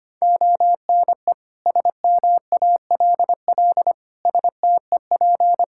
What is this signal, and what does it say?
Morse code, heard as a keyed tone, about 25 wpm, 700 Hz. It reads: ONE SMALL STEP
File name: Sets of 9 Words - 25wpm